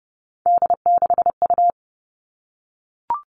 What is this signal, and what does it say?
Morse code, heard as a keyed tone, about 30 wpm, 700 Hz. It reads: D6U
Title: All Cut Numbers - 3-Digit 30wpm